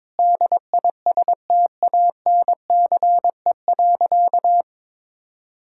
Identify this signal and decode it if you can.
Morse code, heard as a keyed tone, about 22 wpm, 700 Hz. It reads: DISTANCE.
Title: Sentences from Top 1600 Words 22wpm